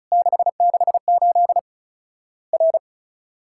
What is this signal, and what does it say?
Morse code, heard as a keyed tone, about 34 wpm, 700 Hz. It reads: 668 R
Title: Worked All Europe DX Contest - QTC Exchange - 35wpm